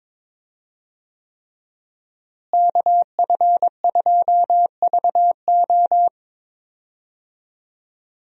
Morse code, heard as a keyed tone, about 22 wpm, 700 Hz. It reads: KF2VO
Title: USA Call Signs - 5 Characters - Repeated 3x - 22wpm